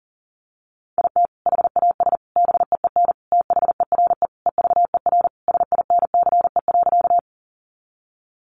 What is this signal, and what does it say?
Morse code, heard as a keyed tone, about 40 wpm, 700 Hz. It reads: IT HAS BEEN THERE EVER SINCE.
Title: Sentences from Top 400 Words 40wpm